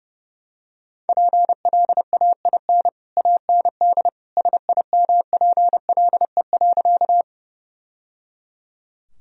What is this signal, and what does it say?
Morse code, heard as a keyed tone, about 30 wpm, 700 Hz. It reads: PLAIN AND SIMPLE.
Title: Sentences from Top 1000 Words - Review Part 2 - 30wpm